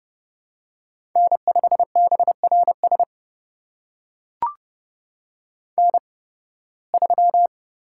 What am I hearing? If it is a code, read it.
Morse code, heard as a keyed tone, about 30 wpm, spaced as for 20 wpm, 700 Hz. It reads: N5BRS N3
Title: USA Call Signs - 5 Characters - 8x Character Spacing - 30wpm